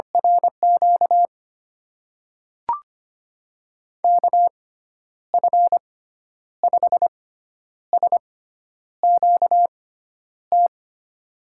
Morse code, heard as a keyed tone, about 25 wpm, spaced as for 10 wpm, 700 Hz. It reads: RQ KF5SQT
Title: USA Call Signs - 6 Characters - 6x Character Spacing - 25wpm